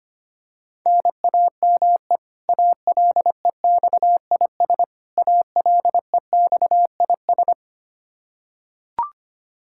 Morse code, heard as a keyed tone, about 25 wpm, 700 Hz. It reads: NAME ALEXIS ALEXIS
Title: Practice QSO - Exchange 1 - Beginning - 25wpm